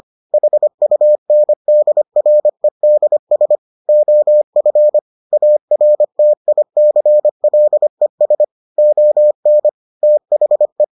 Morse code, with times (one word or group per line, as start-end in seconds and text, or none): 0.3-3.6 HUNDREDS
3.9-5.0 OF
5.3-8.4 ARTICLES
8.8-9.7 ON
10.0-10.8 THE